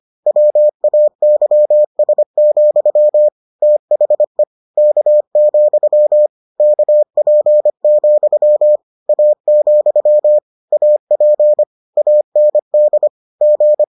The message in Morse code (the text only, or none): WAYS, THE K, KP, A, AP AND G